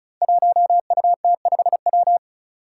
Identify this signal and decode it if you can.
Morse code, heard as a keyed tone, about 36 wpm, 700 Hz. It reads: 1UT5W
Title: Five Challenging Letter-Number Combinations 35wpm